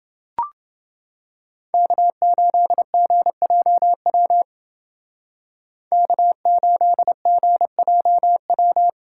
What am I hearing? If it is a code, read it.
Morse code, heard as a keyed tone, about 28 wpm, 700 Hz. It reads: K8GJW K8GJW